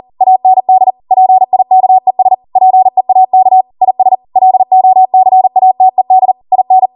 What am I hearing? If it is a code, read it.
Morse code, heard as a keyed tone, about 40 wpm, 750 Hz. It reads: AND PIKES PEAK IS LOCATED IN